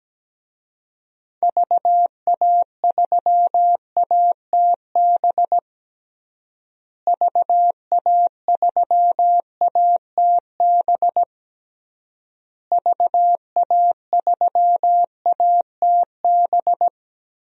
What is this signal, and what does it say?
Morse code, heard as a keyed tone, about 17 wpm, 700 Hz. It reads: VA3ATB VA3ATB VA3ATB
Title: DXCC-100 Call Signs - 6 Characters - Repeated 3x - 17wpm